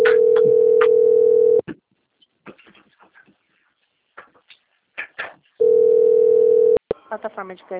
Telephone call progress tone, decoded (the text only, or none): ringback tone